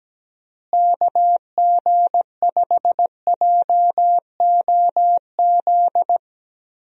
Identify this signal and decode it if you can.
Morse code, heard as a keyed tone, about 17 wpm, 700 Hz. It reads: KG5JOZ
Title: USA Call Signs - 6 Characters - Repeated 2x - 17wpm